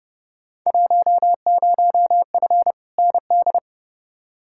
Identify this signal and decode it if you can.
Morse code, heard as a keyed tone, about 30 wpm, 700 Hz. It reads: 10F ND
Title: Field Day Exchanges Encore - 30wpm